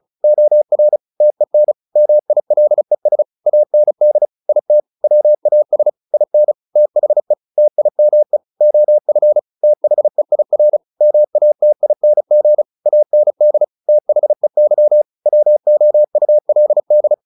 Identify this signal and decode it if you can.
Morse code, heard as a keyed tone, about 34 wpm, 600 Hz. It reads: OR TEN MILES AND IT WAS IN THE TIME OF THEIR MATING AND THEY WOULD
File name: the old man and the sea 2 35wpm